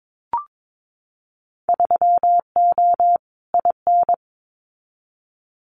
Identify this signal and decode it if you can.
Morse code, heard as a keyed tone, about 22 wpm, 700 Hz. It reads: 3O IN
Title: Winter Field Day Exchanges Encore 22wpm